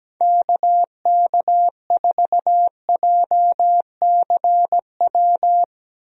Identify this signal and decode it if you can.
Morse code, heard as a keyed tone, about 17 wpm, 700 Hz. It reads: KK4JCW
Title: USA Call Signs - 6 Characters - Repeated 2x - 17wpm